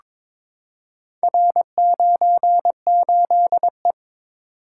Morse code, heard as a keyed tone, about 23 wpm, 700 Hz. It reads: R98E
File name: practice / 22wpm / Four Letter-Number Combinations - 22wpm